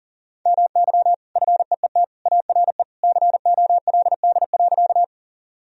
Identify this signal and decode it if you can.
Morse code, heard as a keyed tone, about 40 wpm, 700 Hz. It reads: MY FEET ARE COLD.
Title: Sentences from Top 1000 Words - Review Part 1 - 40wpm